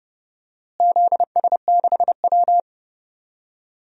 Morse code, heard as a keyed tone, about 30 wpm, 700 Hz. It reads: ZS6W
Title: DXCC-100 Call Signs - 4 Characters - Repeated 3x - 30wpm